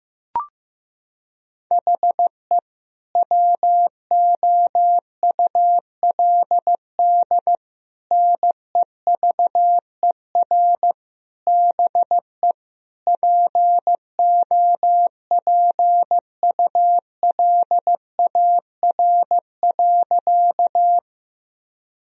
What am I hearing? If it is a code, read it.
Morse code, heard as a keyed tone, about 15 wpm, 700 Hz. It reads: HE WOULD NEVER BE POPULAR.
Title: Sentences from Top 1000 Words - Review Part 3 - 15wpm